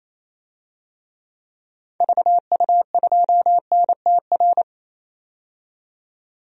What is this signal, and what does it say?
Morse code, heard as a keyed tone, about 28 wpm, 700 Hz. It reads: VU2NTR
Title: DXCC-100 Call Signs - 6 Characters - 28wpm